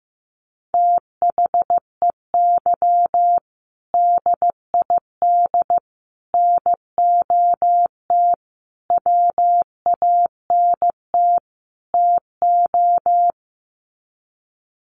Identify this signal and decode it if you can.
Morse code, heard as a keyed tone, about 15 wpm, 700 Hz. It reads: THEY DID NOT WANT TO
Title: Sets of 5 Words 15wpm